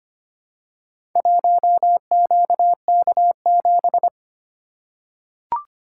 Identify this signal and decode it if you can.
Morse code, heard as a keyed tone, about 25 wpm, 700 Hz. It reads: 1QK7
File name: practice / 25wpm / Four Letter-Number Combinations - 25wpm